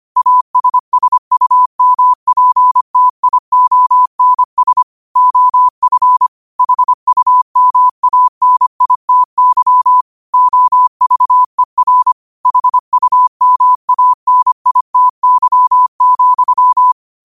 Morse code, heard as a keyed tone, about 25 wpm, 1000 Hz. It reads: ASSUMPTIONS OF HUMANITY OVER HUMANITY,